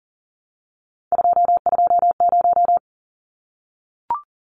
Morse code, heard as a keyed tone, about 40 wpm, 700 Hz. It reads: 220
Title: Numbers - 3-Digit 40wpm